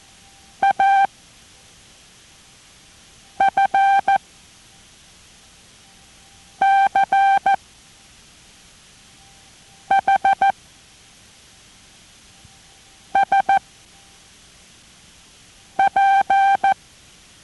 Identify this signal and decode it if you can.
Morse code, heard as a keyed tone, about 14 wpm, 780 Hz. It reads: A F C H S P